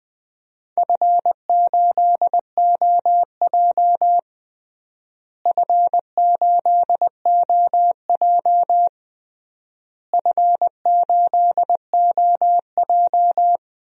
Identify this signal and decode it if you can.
Morse code, heard as a keyed tone, about 20 wpm, 700 Hz. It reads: F8OJ F8OJ F8OJ
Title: DXCC-100 Call Signs - 4 Characters - Repeated 3x - 20wpm